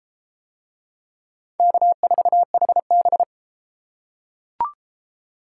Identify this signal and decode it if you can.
Morse code, heard as a keyed tone, about 29 wpm, 700 Hz. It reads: K4HB